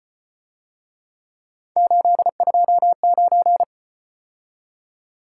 Morse code, heard as a keyed tone, about 32 wpm, 700 Hz. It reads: 829